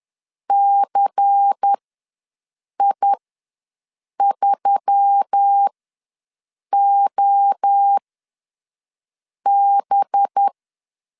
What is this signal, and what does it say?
Morse code, heard as a keyed tone, about 10 wpm, spaced as for 5 wpm, 800 Hz. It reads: CI3OB
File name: LEZIONI / 5 LBVC / 5 LBVC ws 10wpm 3733